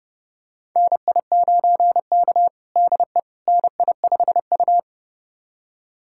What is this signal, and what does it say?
Morse code, heard as a keyed tone, about 30 wpm, 700 Hz. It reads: NI9K DE NI5U
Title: USA Callsign DE Callsign - 4 Characters - 30wpm